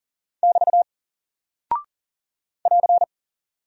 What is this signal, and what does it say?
Morse code, heard as a keyed tone, about 39 wpm, 700 Hz. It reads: <BT> <AR>
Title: Single Character (Letter:Number:Punctuation:Prosign) Alt 40wpm